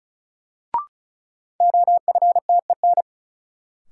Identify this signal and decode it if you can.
Morse code, heard as a keyed tone, about 36 wpm, 700 Hz. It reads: OFTEN
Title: Sentences from Top 300 Words - New Word Drill - 35wpm